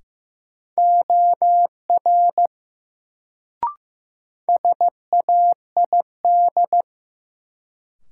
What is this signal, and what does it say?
Morse code, heard as a keyed tone, about 15 wpm, 700 Hz. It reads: OR SAID
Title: Sets of 2 Words - New Word Drill - 15wpm